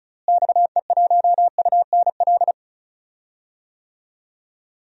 Morse code, heard as a keyed tone, about 34 wpm, 700 Hz. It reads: XE1UNL
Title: DXCC-100 Call Signs - 6 Characters - Repeated 3x - 35wpm